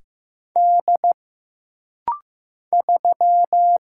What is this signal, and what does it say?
Morse code, heard as a keyed tone, about 15 wpm, 700 Hz. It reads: D 3
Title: Single Letter-Number Alt - Rapid-Fire 15wpm